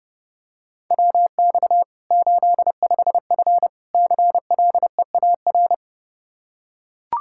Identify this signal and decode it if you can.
Morse code, heard as a keyed tone, about 30 wpm, 700 Hz. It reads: WX 85F CLEAR
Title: Practice QSO - Exchange 2 - Info - 30wpm